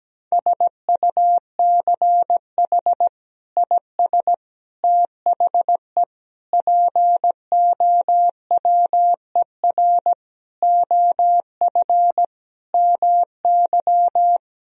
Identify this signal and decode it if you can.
Morse code, heard as a keyed tone, about 17 wpm, 700 Hz. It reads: SUCH IS THE POWER OF MY
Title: Sentences from Top 1000 Words - Review Part 2 - 17wpm